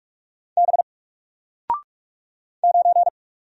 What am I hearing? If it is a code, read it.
Morse code, heard as a keyed tone, about 46 wpm, 700 Hz. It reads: B 9
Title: Single Letter-Number 45wpm